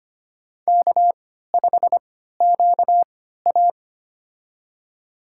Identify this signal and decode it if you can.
Morse code, heard as a keyed tone, about 25 wpm, spaced as for 14 wpm, 700 Hz. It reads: K5QA
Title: USA Call Signs - 4 Characters - 3x Character Spacing - 25wpm